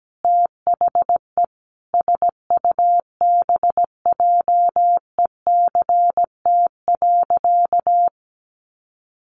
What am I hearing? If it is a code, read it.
Morse code, heard as a keyed tone, about 17 wpm, 700 Hz. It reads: THE SUBJECT.
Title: Sentences from Top 1400 Words 17wpm